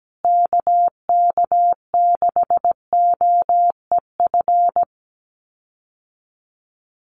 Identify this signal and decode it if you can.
Morse code, heard as a keyed tone, about 17 wpm, 700 Hz. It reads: KK6OEF
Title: USA Call Signs Repeated 3x 17wpm